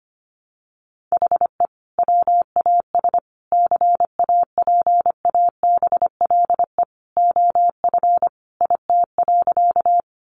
Morse code, heard as a keyed tone, about 25 wpm, 700 Hz. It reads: HE WAS CAPABLE OF IT.
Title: Sentences from Top 1200 Words 25wpm